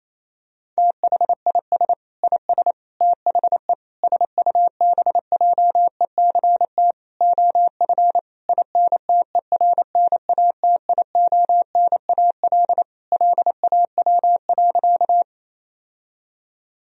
Morse code, heard as a keyed tone, about 28 wpm, 700 Hz. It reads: THIS IS THE SUBJECT OF INTERNATIONAL LAW.
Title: Sentences from Top 1000 Words - Review Part 4 - 28wpm